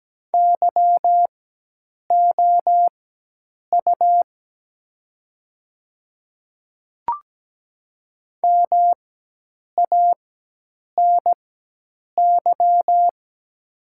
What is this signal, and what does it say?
Morse code, heard as a keyed tone, about 17 wpm, spaced as for 8 wpm, 700 Hz. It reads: YOU MANY
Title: 4x Character Spacing - Top 300 Words - 17wpm